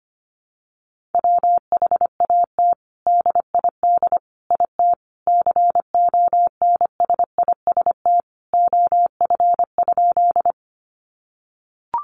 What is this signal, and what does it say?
Morse code, heard as a keyed tone, about 25 wpm, 700 Hz. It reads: WHAT DID IT CONSIST OF?
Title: Sentences from Top 1400 Words 25wpm